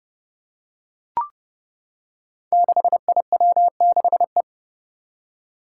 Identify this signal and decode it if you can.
Morse code, heard as a keyed tone, about 32 wpm, 700 Hz. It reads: 6IW6E
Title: Five Challenging Letter-Number Combinations 30wpm